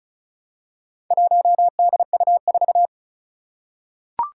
Morse code, heard as a keyed tone, about 34 wpm, 700 Hz. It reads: 1DU4
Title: Four Challenging Letter-Number Combinations 35wpm